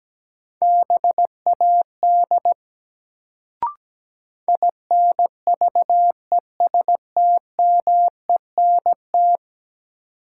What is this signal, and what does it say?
Morse code, heard as a keyed tone, about 17 wpm, 700 Hz. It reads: BAD INVESTMENT
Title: Sentences from Top 700 Words - New Word Drill - 17wpm